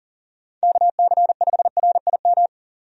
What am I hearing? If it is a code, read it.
Morse code, heard as a keyed tone, about 40 wpm, 700 Hz. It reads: KC5RIM
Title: USA Call Signs - 6 Characters - 40wpm